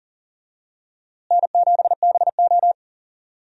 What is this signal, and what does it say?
Morse code, heard as a keyed tone, about 40 wpm, 700 Hz. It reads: N7BO
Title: USA Call Signs - 4 Characters - Repeated 2x - 40wpm